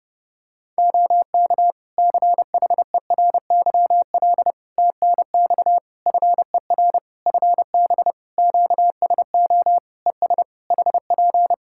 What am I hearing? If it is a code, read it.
Morse code, heard as a keyed tone, about 30 wpm, 700 Hz. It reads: OK CHERYL TNX FER FB QSO ES HP
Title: Practice QSO - Exchange 4 - Ending - 30wpm